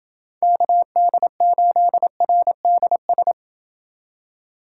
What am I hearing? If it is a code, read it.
Morse code, heard as a keyed tone, about 28 wpm, 700 Hz. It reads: KD8RDS